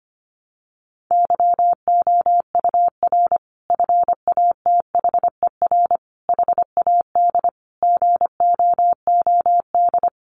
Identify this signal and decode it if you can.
Morse code, heard as a keyed tone, about 25 wpm, 700 Hz. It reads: YOUR FATHER HAD GOOD
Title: Sentences from Top 1100 Words 25wpm